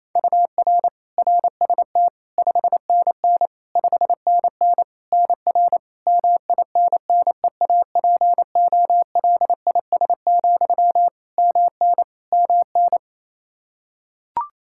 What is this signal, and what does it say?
Morse code, heard as a keyed tone, about 28 wpm, 700 Hz. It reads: UR RST 5NN 5NN NR MINNEAPOLIS, MN MN
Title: Practice QSO - Exchange 1 - Beginning - 28wpm